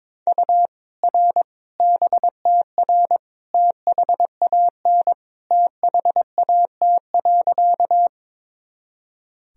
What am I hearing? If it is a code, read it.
Morse code, heard as a keyed tone, about 22 wpm, 700 Hz. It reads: U R BTR THAN THAT.
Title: Sentences from Top 400 Words and Abbreviations 22wpm